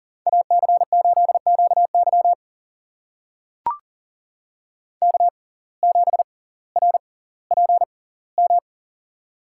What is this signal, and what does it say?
Morse code, heard as a keyed tone, about 39 wpm, spaced as for 20 wpm, 700 Hz. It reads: AC8QY K7RPM
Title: USA Call Signs - 5 Characters - 6x Character Spacing - 40wpm